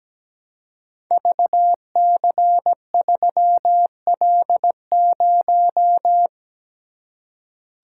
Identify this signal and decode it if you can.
Morse code, heard as a keyed tone, about 17 wpm, 700 Hz. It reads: VC3L0
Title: Code Groups - 5 Letter:Numbers - 17wpm